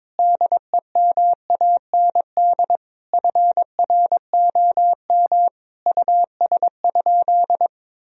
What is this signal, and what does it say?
Morse code, heard as a keyed tone, about 22 wpm, 700 Hz. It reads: DEMAND FROM US?
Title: Sentences from Top 1000 Words - Review Part 4 - 22wpm